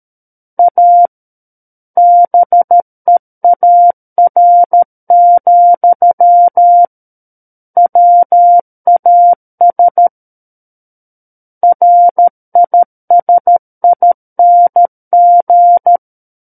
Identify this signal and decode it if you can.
Morse code, heard as a keyed tone, about 13 wpm, 700 Hz. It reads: A BEAR, WAS RISING
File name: Book One / 04-Chapter 4 - 13 WPM